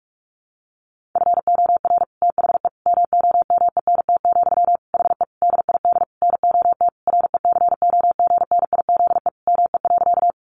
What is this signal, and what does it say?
Morse code, heard as a keyed tone, about 44 wpm, 700 Hz. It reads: FOR THE MOMENT, HE DID NOT RECOGNIZE ME.
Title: Sentences from Top 1400 Words 45wpm